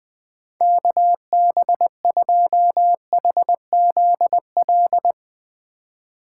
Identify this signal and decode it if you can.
Morse code, heard as a keyed tone, about 20 wpm, 700 Hz. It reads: KB2HZL
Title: Speed Racing - USA Call Signs 17wpm